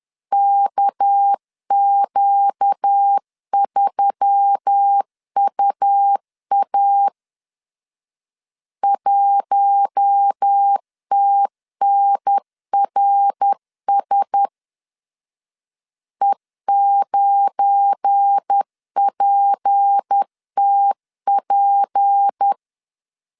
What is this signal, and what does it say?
Morse code, heard as a keyed tone, about 10 wpm, 800 Hz. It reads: KQ3UA 1TNRS E9PTP
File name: LEZIONI / 8 YJ45 / 8 YJ 10wpm 2651